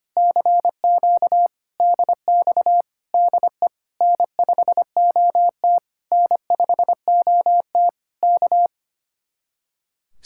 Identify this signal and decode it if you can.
Morse code, heard as a keyed tone, about 28 wpm, 700 Hz. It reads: CQ DX DE N5OT N5OT K
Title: Speed-Racing Common QSO Phrases - 25wpm